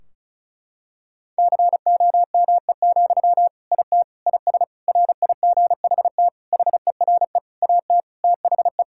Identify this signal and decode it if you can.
Morse code, heard as a keyed tone, about 36 wpm, 700 Hz. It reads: COME, IT IS RIGHT HERE AT THE
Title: Sentences from Top 1600 Words 35wpm